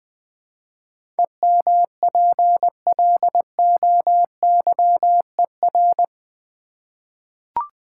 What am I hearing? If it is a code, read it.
Morse code, heard as a keyed tone, about 20 wpm, 700 Hz. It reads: EMPLOYER
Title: Eight Letter Words - 20wpm